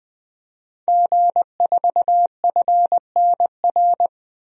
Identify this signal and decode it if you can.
Morse code, heard as a keyed tone, about 20 wpm, 700 Hz. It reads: G4FNR
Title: DXCC-100 Call Signs - 5 Characters - 20wpm